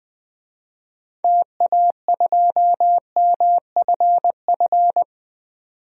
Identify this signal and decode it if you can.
Morse code, heard as a keyed tone, about 20 wpm, 700 Hz. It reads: TA2MFF
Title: DXCC-100 Call Signs - 6 Characters - 20wpm